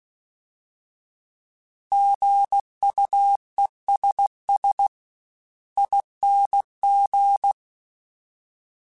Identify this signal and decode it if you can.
Morse code, heard as a keyed tone, about 16 wpm, spaced as for 14 wpm, 800 Hz. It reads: GUESSING
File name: Alphabetical / Alphabetical mp3s / Gs @15